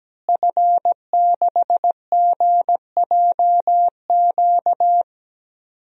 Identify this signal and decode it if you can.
Morse code, heard as a keyed tone, about 17 wpm, 700 Hz. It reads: F6GJQ
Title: DXCC-100 Call Signs - 5 Characters - Repeated 2x - 17wpm